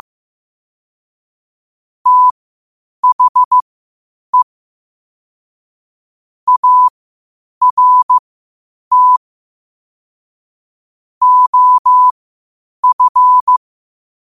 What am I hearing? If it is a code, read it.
Morse code, heard as a keyed tone, about 15 wpm, spaced as for 8 wpm, 1000 Hz. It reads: THE ART OF
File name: qotdfiles / 8wpm